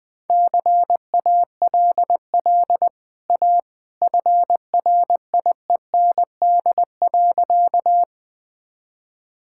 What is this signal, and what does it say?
Morse code, heard as a keyed tone, about 20 wpm, 700 Hz. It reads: CALL A FRIEND.
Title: Easy Sentences with Hints - 20wpm